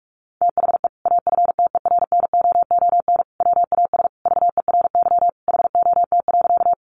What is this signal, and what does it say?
Morse code, heard as a keyed tone, about 44 wpm, 700 Hz. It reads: THE AFTERNOON WAS VERY HOT.
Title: Sentences from Top 1500 Words 45wpm